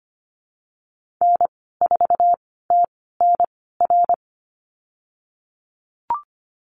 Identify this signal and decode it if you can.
Morse code, heard as a keyed tone, about 25 wpm, spaced as for 16 wpm, 700 Hz. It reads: N4TNR
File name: Lesson 15g - Number 5 - Call Signs 25wpm